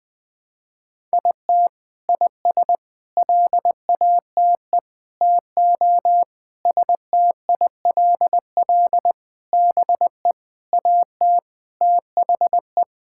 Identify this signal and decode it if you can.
Morse code, heard as a keyed tone, about 21 wpm, 700 Hz. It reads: IT IS LATE TO STILL BE AT THE
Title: Sentences from Top 1000 Words - Review Part 5 - 20wpm